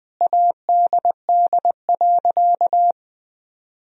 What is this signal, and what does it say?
Morse code, heard as a keyed tone, about 20 wpm, 700 Hz. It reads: ADD.
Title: Sentences from Top 1000 Words - Review Part 1 - 20wpm